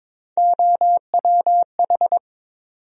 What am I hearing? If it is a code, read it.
Morse code, heard as a keyed tone, about 22 wpm, 700 Hz. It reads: OWH
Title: USA Call Sign Postfixes - 3 Letters 22wpm